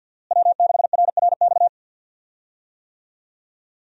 Morse code, heard as a keyed tone, about 52 wpm, 700 Hz. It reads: W6RRX
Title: USA Call Signs - 5 Characters - Repeated 2x - 50wpm